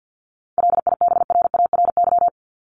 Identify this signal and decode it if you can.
Morse code, heard as a keyed tone, about 50 wpm, 700 Hz. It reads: LIBRARY
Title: Top 501-750 Words 50wpm